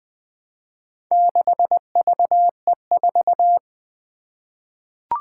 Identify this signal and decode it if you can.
Morse code, heard as a keyed tone, about 20 wpm, 700 Hz. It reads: 6VE4
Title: Four Challenging Letter-Number Combinations 20wpm